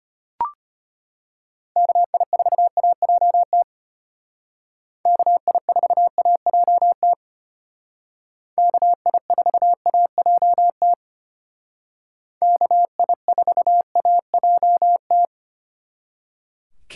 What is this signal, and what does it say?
Morse code, heard as a keyed tone, about 32 wpm, 700 Hz. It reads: KI4AJT KI4AJT KI4AJT KI4AJT